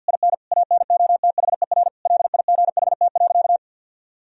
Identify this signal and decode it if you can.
Morse code, heard as a keyed tone, about 49 wpm, 700 Hz. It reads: IN ANOTHER LIGHT.
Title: Sentences from Top 1000 Words - Review Part 1 - 50wpm